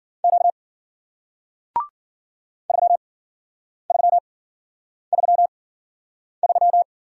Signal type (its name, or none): Morse code